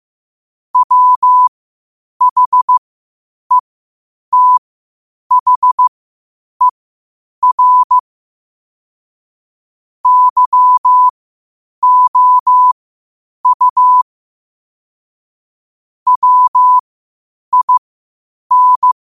Morse code, heard as a keyed tone, about 15 wpm, spaced as for 8 wpm, 1000 Hz. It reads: WHETHER YOU WIN
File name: 8wpm